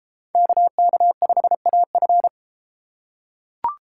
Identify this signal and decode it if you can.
Morse code, heard as a keyed tone, about 30 wpm, 700 Hz. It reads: KK5AF